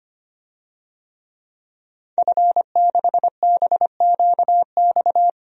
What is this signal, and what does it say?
Morse code, heard as a keyed tone, about 25 wpm, 700 Hz. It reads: F6BQX
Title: DXCC-100 Call Signs - 5 Characters - 25wpm